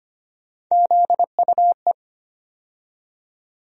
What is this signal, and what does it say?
Morse code, heard as a keyed tone, about 25 wpm, 700 Hz. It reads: ZUE
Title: USA Call Sign Postfixes - 3 Letters 25wpm